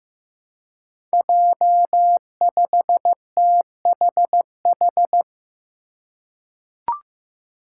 Morse code, heard as a keyed tone, about 15 wpm, 700 Hz. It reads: J5THH
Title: Five Challenging Letter-Number Combinations 15wpm